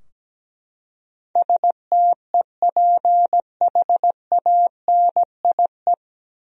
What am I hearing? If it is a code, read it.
Morse code, heard as a keyed tone, about 17 wpm, 700 Hz. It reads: STEPHANIE
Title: USA Common Names - 17wpm